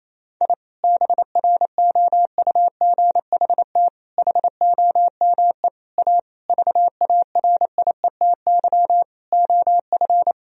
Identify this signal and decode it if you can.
Morse code, heard as a keyed tone, about 28 wpm, 700 Hz. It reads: I BROUGHT HOME A VARIETY OF
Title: Sentences from Top 1000 Words - Review Part 4 - 28wpm